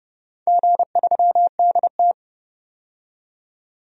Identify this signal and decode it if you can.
Morse code, heard as a keyed tone, about 29 wpm, 700 Hz. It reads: G3DT
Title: DXCC-100 Call Signs - 4 Characters - 30wpm